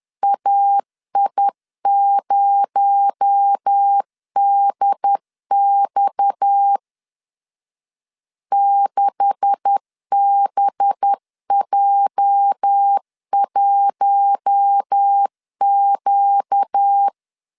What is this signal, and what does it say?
Morse code, heard as a keyed tone, about 10 wpm, 800 Hz. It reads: AI0DX 6BJ1Q
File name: CW / DET / 1 9 1500C S A 10wpm 4511